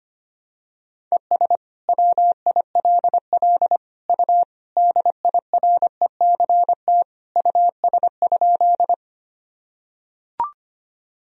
Morse code, heard as a keyed tone, about 25 wpm, 700 Hz. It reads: ES WILL U DIRECT US?
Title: Sentences from Top 400 Words and Abbreviations 25wpm